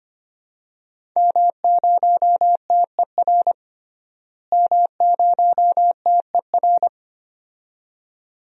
Morse code, heard as a keyed tone, about 25 wpm, 700 Hz. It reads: M0TER M0TER
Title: DXCC-100 Call Signs - 5 Characters - Repeated 2x - 25wpm